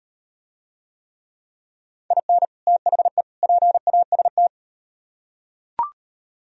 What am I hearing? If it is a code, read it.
Morse code, heard as a keyed tone, about 38 wpm, 700 Hz. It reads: IN THE PAST